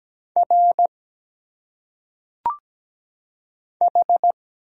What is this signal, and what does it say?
Morse code, heard as a keyed tone, about 17 wpm, 700 Hz. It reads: R H
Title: Single Letters - Phonetic Pronunciation 17wpm